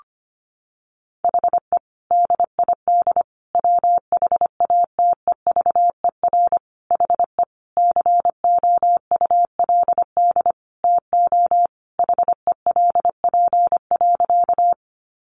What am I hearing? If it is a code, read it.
Morse code, heard as a keyed tone, about 25 wpm, 700 Hz. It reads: HE DID WHATEVER HE COULD TO HELP.
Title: Sentences from Top 1100 Words 25wpm